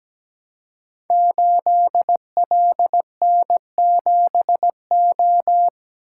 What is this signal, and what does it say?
Morse code, heard as a keyed tone, about 17 wpm, 700 Hz. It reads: 8LN7O
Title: Code Groups - 5 Letter:Numbers - 17wpm